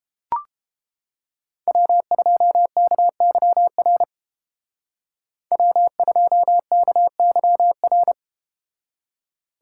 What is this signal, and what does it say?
Morse code, heard as a keyed tone, about 32 wpm, 700 Hz. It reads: W2KYR W2KYR